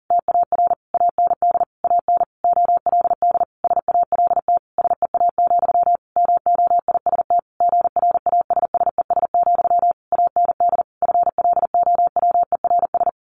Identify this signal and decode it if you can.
Morse code, heard as a keyed tone, about 40 wpm, 700 Hz. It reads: TAR AND AN OLD SALT SEA, MOIST GRASSES, AND FLOWERS
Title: A Sound Of Thunder 40wpm